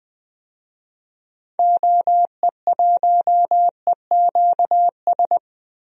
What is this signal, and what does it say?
Morse code, heard as a keyed tone, about 20 wpm, 700 Hz. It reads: OE1EQS
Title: DXCC-100 Call Signs - 6 Characters - Repeated 3x - 20wpm